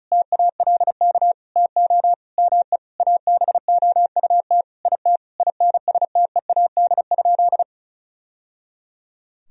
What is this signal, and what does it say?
Morse code, heard as a keyed tone, about 36 wpm, 700 Hz. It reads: TALK TO ME ABOUT IT INSTEAD?
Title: Sentences from Top 1000 Words - Review Part 4 - 35wpm